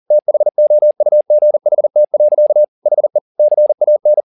Morse code, heard as a keyed tone, about 40 wpm, 600 Hz. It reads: THOUGHT. HE CAN